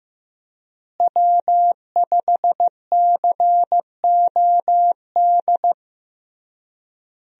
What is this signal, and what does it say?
Morse code, heard as a keyed tone, about 16 wpm, 700 Hz. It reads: W5COD
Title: USA Call Signs - 5 Characters - Repeated 3x - 15wpm